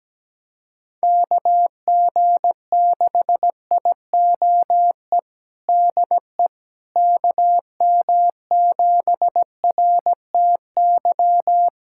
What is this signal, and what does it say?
Morse code, heard as a keyed tone, about 17 wpm, 700 Hz. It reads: KG6IOE DE KM7RTY
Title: USA Callsign DE Callsign 17wpm